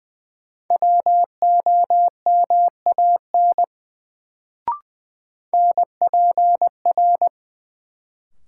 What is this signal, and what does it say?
Morse code, heard as a keyed tone, about 20 wpm, 700 Hz. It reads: WOMAN NPR
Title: Sets of 5 Words - New Word Drill - 20wpm